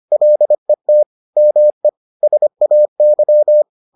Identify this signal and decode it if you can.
Morse code, heard as a keyed tone, about 25 wpm, 600 Hz. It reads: LET ME SAY